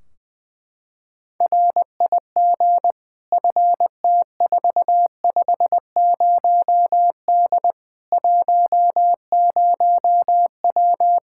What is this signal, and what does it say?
Morse code, heard as a keyed tone, about 20 wpm, 700 Hz. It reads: RIG FT450D 10W
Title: K1USN SST QSO Practice 20wpm